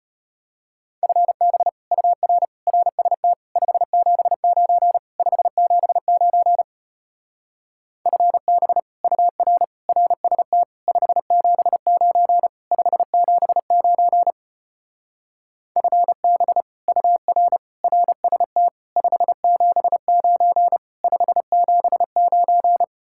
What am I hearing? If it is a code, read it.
Morse code, heard as a keyed tone, about 33 wpm, 700 Hz. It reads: FB UR RST 579 579 FB UR RST 579 579 FB UR RST 579 579